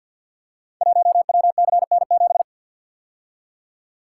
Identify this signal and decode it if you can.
Morse code, heard as a keyed tone, about 48 wpm, 690 Hz. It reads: 1WCN7
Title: Code Groups - 5 Letter:Numbers - 50wpm